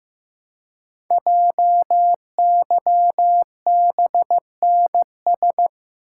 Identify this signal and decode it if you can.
Morse code, heard as a keyed tone, about 15 wpm, 700 Hz. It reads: JYBNS
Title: Code Groups - 5 Letters - 15wpm